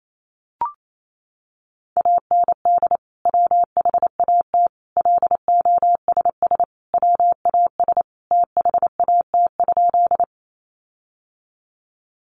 Morse code, heard as a keyed tone, about 28 wpm, 700 Hz. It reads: AND WHAT LOSS WAS THAT?
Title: Sentences from Top 1400 Words 28wpm